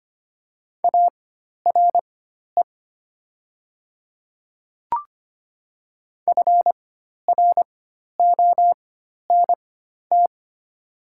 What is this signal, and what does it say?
Morse code, heard as a keyed tone, about 25 wpm, spaced as for 12 wpm, 700 Hz. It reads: ARE FRONT
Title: 4x Character Spacing - Top 500 Words - 25wpm